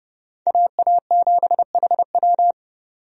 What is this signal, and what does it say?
Morse code, heard as a keyed tone, about 30 wpm, 700 Hz. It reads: AA7HW
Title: USA Call Signs - 5 Characters - 30wpm